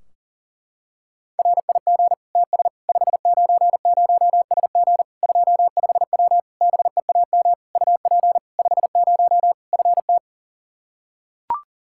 Morse code, heard as a keyed tone, about 40 wpm, 700 Hz. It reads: RIG TS 590SG 25W BEAM UP 50 FT
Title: Practice QSO - Exchange 2 - Info - 40wpm